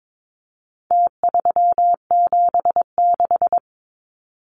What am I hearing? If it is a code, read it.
Morse code, heard as a keyed tone, about 22 wpm, 700 Hz. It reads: T376
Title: Numbers - Contest Style - 4-Digit 22wpm